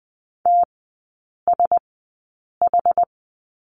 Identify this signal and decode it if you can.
Morse code, heard as a keyed tone, about 20 wpm, spaced as for 8 wpm, 690 Hz. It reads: TSH
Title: 081 Lesson 14e - Number 2 - Three Characters 20wpm